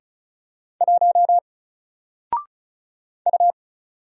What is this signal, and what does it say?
Morse code, heard as a keyed tone, about 34 wpm, 700 Hz. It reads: 1 U
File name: Single-Letter-Number-35wpm